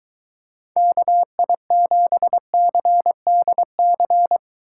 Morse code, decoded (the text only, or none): KI7CDC